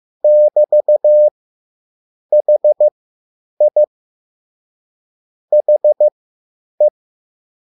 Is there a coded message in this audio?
Yes: Morse code from 0.2 to 6.9 s, heard as a keyed tone, about 15 wpm, spaced as for 8 wpm, 600 Hz. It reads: <BT>HI HE